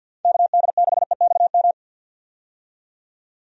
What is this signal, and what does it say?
Morse code, heard as a keyed tone, about 47 wpm, 700 Hz. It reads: KD6EXM
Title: USA Call Signs - 6 Characters - 50wpm